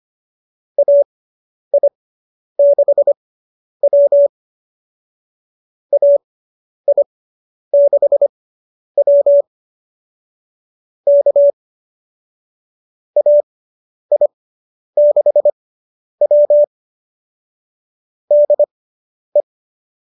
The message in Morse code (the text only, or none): AI6W AI6W K AI6W DE